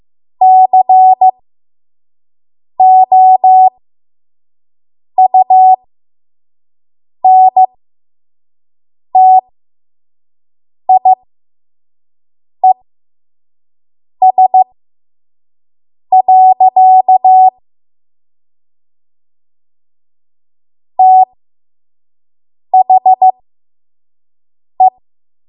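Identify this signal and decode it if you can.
Morse code, heard as a keyed tone, about 15 wpm, spaced as for 5 wpm, 750 Hz. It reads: COUNTIES. THE
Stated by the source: 5 WPM Morse code practice.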